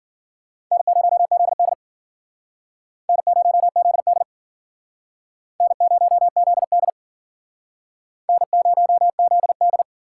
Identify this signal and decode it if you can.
Morse code, heard as a keyed tone, about 50 wpm, 700 Hz. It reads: N0ZD N0ZD N0ZD N0ZD
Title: Speed Racing - USA 1x2 Call Signs 40wpm